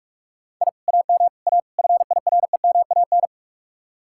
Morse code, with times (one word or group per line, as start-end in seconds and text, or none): 0.6-0.7 I
0.9-1.3 AM
1.5-1.6 A
1.8-3.3 FIREMAN